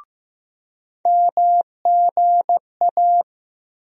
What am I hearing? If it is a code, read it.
Morse code, heard as a keyed tone, about 15 wpm, 700 Hz. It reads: MGA
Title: USA Call Sign Postfixes - 3 Letters 15wpm